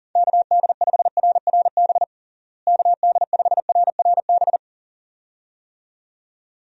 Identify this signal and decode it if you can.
Morse code, heard as a keyed tone, about 40 wpm, 700 Hz. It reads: KD5RRB KD5RRB
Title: USA Call Signs - 6 Characters - Repeated 2x - 40wpm